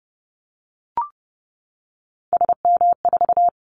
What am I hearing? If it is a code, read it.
Morse code, heard as a keyed tone, about 30 wpm, 700 Hz. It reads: SM4
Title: Callsign Prefixes for Top 25 DXCC - 30wpm